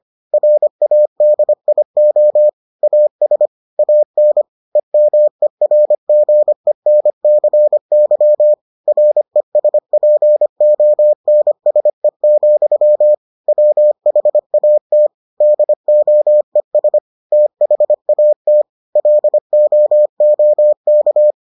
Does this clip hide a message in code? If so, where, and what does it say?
Yes: Morse code from 0.3 to 21.3 s, heard as a keyed tone, about 25 wpm, 600 Hz. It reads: RADIO AS AN EMERGENCY RESPONSE, WHAT DOES THAT LOOK